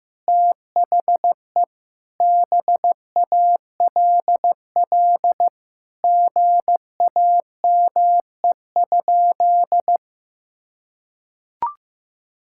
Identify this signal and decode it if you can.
Morse code, heard as a keyed tone, about 15 wpm, 700 Hz. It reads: THE BALL GAME?
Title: Sentences from Top 1000 Words - Review Part 6 - 15wpm